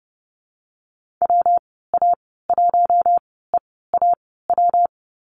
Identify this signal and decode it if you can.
Morse code, heard as a keyed tone, about 30 wpm, spaced as for 17 wpm, 700 Hz. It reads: WA1EAW
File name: Lesson 37g - Letter x - Call Signs 30wpm